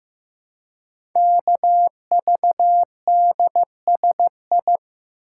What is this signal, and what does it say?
Morse code, heard as a keyed tone, about 15 wpm, 700 Hz. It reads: KVDSI
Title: Code Groups - 5 Letters - 15wpm